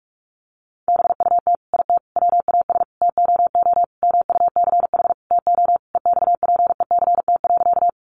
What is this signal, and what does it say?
Morse code, heard as a keyed tone, about 44 wpm, 700 Hz. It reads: BUT IT WAS TOO MUCH TO EXPECT.
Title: Sentences from Top 1100 Words 45wpm